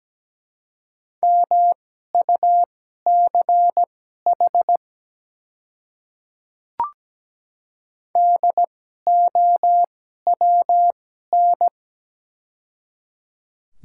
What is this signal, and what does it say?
Morse code, heard as a keyed tone, about 17 wpm, spaced as for 12 wpm, 700 Hz. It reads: MUCH DOWN
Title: 2x Character Spacing - Top 300 Words - 17wpm